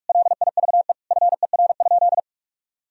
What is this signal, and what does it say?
Morse code, heard as a keyed tone, about 46 wpm, 700 Hz. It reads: LIVE FER?
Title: Sentences from Top 200 Words and Abbreviations 45wpm